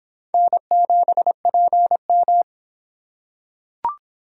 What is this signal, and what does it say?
Morse code, heard as a keyed tone, about 25 wpm, 700 Hz. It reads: N7PM